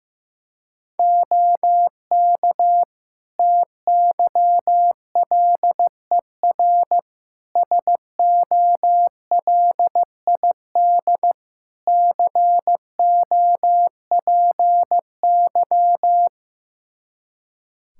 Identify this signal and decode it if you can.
Morse code, heard as a keyed tone, about 16 wpm, 700 Hz. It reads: OK TYLER SOLID COPY
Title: Practice QSO - Exchange 3 - More Info - 15wpm